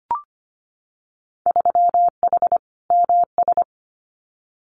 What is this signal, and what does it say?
Morse code, heard as a keyed tone, about 25 wpm, 700 Hz. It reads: 3H MS
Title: Winter Field Day Exchanges Encore 25wpm